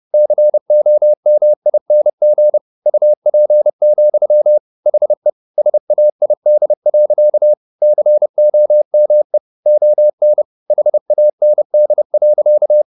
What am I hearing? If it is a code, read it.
Morse code, heard as a keyed tone, about 30 wpm, 600 Hz. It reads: COMING UP, HE SAID. COME ON HAND.
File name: the old man and the sea / the old man and the sea 1 30wpm